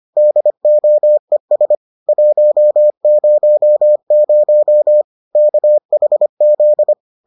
Morse code, heard as a keyed tone, about 25 wpm, 600 Hz. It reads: DOES 100 KHZ